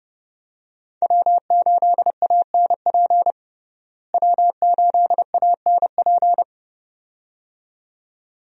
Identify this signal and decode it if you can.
Morse code, heard as a keyed tone, about 30 wpm, 700 Hz. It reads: W8ANP W8ANP
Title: USA Call Signs - 5 Characters - Repeated 2x - 30wpm